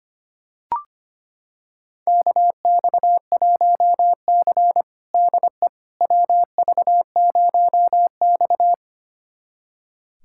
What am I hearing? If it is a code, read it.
Morse code, heard as a keyed tone, about 25 wpm, 700 Hz. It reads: KX1C DE WV0X
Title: USA Callsign DE Callsign - 4 Characters - 25wpm